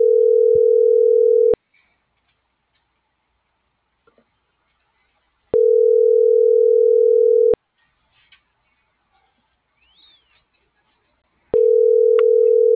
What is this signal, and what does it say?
Ringback tone (North America), 440 + 480 Hz.